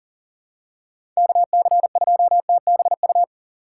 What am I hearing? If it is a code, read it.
Morse code, heard as a keyed tone, about 38 wpm, 700 Hz. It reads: KC2TBU
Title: USA Call Signs - 6 Characters - 6x Character Spacing - 40wpm